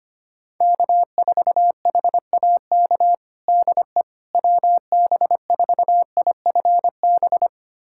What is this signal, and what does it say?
Morse code, heard as a keyed tone, about 25 wpm, 700 Hz. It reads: K4HAK DE WB4IFB
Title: USA Callsign DE Callsign 25wpm